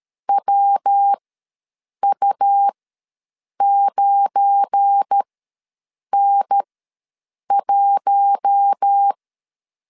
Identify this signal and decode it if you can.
Morse code, heard as a keyed tone, about 13 wpm, spaced as for 7 wpm, 800 Hz. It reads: WU9N1
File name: LEZIONI / 9 2867 ws 12wpm 3431